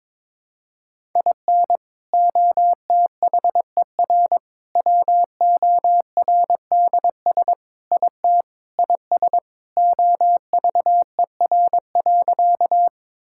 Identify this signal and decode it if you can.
Morse code, heard as a keyed tone, about 22 wpm, 700 Hz. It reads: IN OTHER WORDS IT IS OVER.
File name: practice / 22wpm / Sentences with 3-Word Phrases - 22wpm